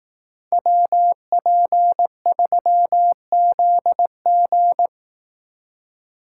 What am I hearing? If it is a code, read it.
Morse code, heard as a keyed tone, about 18 wpm, 700 Hz. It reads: WP3ZG
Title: Speed Racing - USA Call Signs 15wpm